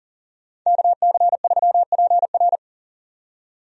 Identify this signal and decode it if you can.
Morse code, heard as a keyed tone, about 40 wpm, 700 Hz. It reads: KC3PR
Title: USA Call Signs - 5 Characters - 40wpm